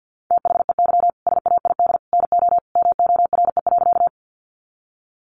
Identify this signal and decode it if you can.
Morse code, heard as a keyed tone, about 48 wpm, 700 Hz. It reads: THEY SAID NO MORE.
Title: Sentences from Top 100 Words 50wpm